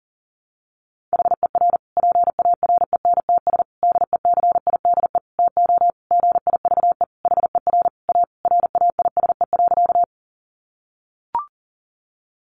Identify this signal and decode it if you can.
Morse code, heard as a keyed tone, about 40 wpm, 700 Hz. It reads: HER PARENTS DECIDE TO GIVE HER A RAISE.
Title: Easy Phrases 40wpm